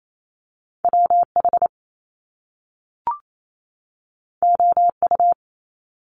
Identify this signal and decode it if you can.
Morse code, heard as a keyed tone, about 28 wpm, 700 Hz. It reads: W H O U
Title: Top Two Letter Combinations in Words 28wpm